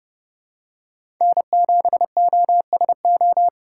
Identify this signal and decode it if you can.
Morse code, heard as a keyed tone, about 32 wpm, 700 Hz. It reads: N7OSO
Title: USA Call Signs - 5 Characters - 3x Character Spacing - 30wpm